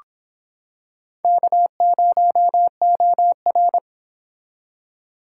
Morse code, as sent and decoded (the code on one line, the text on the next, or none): -.- ----- --- .-.
K0OR